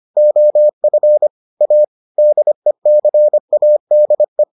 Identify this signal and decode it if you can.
Morse code, heard as a keyed tone, about 25 wpm, 600 Hz. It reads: OF A DECADE